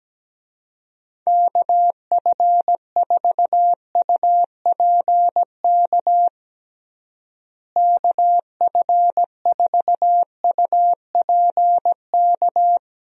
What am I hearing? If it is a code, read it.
Morse code, heard as a keyed tone, about 17 wpm, 700 Hz. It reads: KF4UPK KF4UPK
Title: USA Call Signs - 6 Characters - Repeated 3x - 17wpm